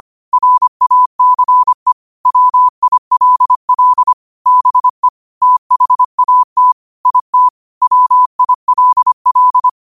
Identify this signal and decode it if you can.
Morse code, heard as a keyed tone, about 25 wpm, 1000 Hz. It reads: RACE WILL BE THAT IT WILL
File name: qotdfiles / 25wpm